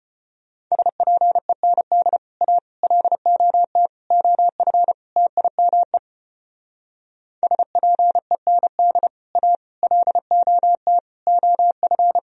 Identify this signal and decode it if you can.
Morse code, heard as a keyed tone, about 32 wpm, 700 Hz. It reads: SPEND A LOT OF TIME SPEND A LOT OF